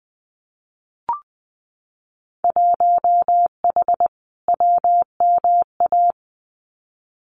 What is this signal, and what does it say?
Morse code, heard as a keyed tone, about 20 wpm, 700 Hz. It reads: 1H WMA
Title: Winter Field Day Exchanges 20wpm